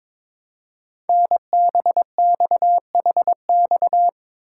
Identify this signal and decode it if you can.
Morse code, heard as a keyed tone, about 22 wpm, 700 Hz. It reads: NBXHX
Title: Code Groups - 5 Letters - 22wpm